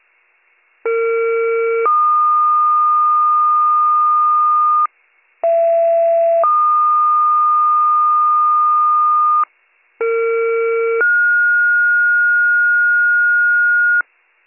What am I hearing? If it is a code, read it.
Two-tone page, heard as a one-second tone, then a three-second tone, about 460 then 1220 Hz, about 670 then 1220 Hz, about 460 then 1430 Hz. It is heard from 0.8 to 14.0 s.